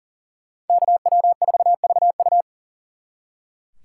Morse code, heard as a keyed tone, about 49 wpm, 700 Hz. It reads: KW4VU
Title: Speed Racing - USA 2x2 Call Signs 40wpm